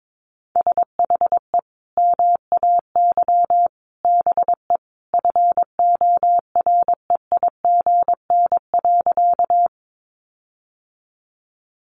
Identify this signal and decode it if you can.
Morse code, heard as a keyed tone, about 22 wpm, 700 Hz. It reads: SHE MAY BE FOREIGN.
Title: Sentences from Top 1400 Words 22wpm